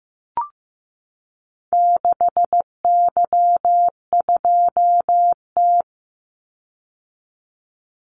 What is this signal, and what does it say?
Morse code, heard as a keyed tone, about 15 wpm, 700 Hz. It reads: 6Y2T
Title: Call Signs Repeated 3x - 15wpm